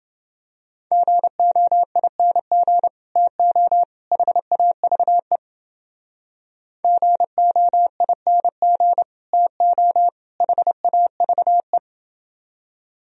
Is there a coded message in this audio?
Yes: Morse code from 0.9 to 11.8 s, heard as a keyed tone, about 30 wpm, 700 Hz. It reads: GOING TO HAVE GOING TO HAVE